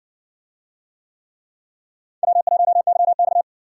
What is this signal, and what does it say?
Morse code, heard as a keyed tone, about 58 wpm, 700 Hz. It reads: W1YX